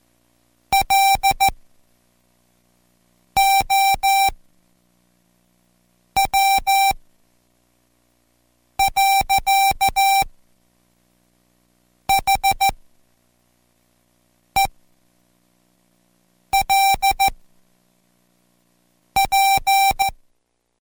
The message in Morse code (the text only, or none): LOW. HELP